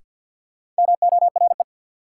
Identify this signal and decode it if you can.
Morse code, heard as a keyed tone, about 54 wpm, 700 Hz. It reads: MORE
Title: Sentences from Top 100 Words - New Word Drill - 50wpm